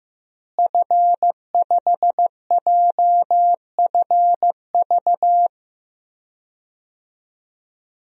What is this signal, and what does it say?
Morse code, heard as a keyed tone, about 15 wpm, 700 Hz. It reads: F5JFV
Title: DXCC-100 Call Signs - 5 Characters - Repeated 2x - 15wpm